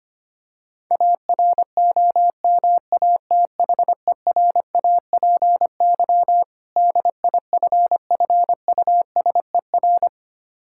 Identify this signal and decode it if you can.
Morse code, heard as a keyed tone, about 25 wpm, 700 Hz. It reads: AROMATHERAPY DIFFUSER
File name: practice / 25wpm / Difficult Words with Hints - 25wpm